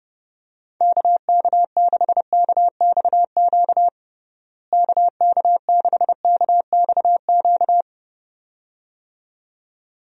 Morse code, heard as a keyed tone, about 32 wpm, 700 Hz. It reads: KK6KXQ KK6KXQ
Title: USA Call Signs - 6 Characters - Repeated 2x - 30wpm